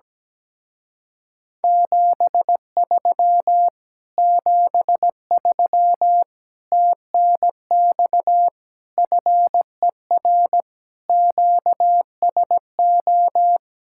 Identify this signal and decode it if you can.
Morse code, heard as a keyed tone, about 17 wpm, 700 Hz. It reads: 73 73 TNX FER QSO
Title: Speed-Racing Common QSO Phrases - 17wpm